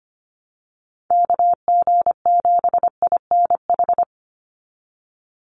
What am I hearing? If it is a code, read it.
Morse code, heard as a keyed tone, about 25 wpm, 700 Hz. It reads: KG7INH
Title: USA Call Signs 25wpm